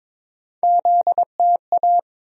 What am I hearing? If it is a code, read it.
Morse code, heard as a keyed tone, about 22 wpm, 700 Hz. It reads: ZTA
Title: USA Call Sign Postfixes - 3 Letters 22wpm